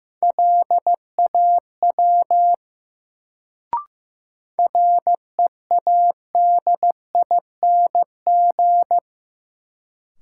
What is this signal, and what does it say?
Morse code, heard as a keyed tone, about 15 wpm, 700 Hz. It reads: LAW READING
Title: Sentences from Top 600 Words - New Word Drill - 15wpm